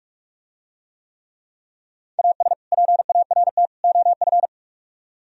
Morse code, heard as a keyed tone, about 47 wpm, 700 Hz. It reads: AS PART OF